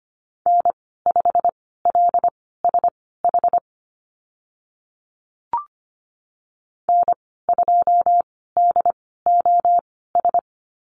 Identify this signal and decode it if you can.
Morse code, heard as a keyed tone, about 25 wpm, spaced as for 16 wpm, 700 Hz. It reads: N5LSH N2DOS
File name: Lesson 15g - Number 5 - Call Signs 25wpm